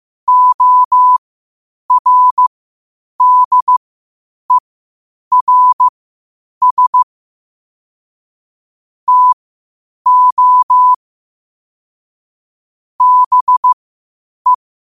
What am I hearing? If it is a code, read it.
Morse code, heard as a keyed tone, about 15 wpm, spaced as for 8 wpm, 1000 Hz. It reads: ORDERS TO BE
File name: qotdfiles / 8wpm